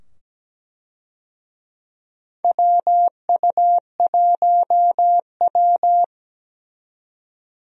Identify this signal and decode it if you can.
Morse code, heard as a keyed tone, about 17 wpm, 700 Hz. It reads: WU1W
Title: USA Call Signs - 4 Characters - 17wpm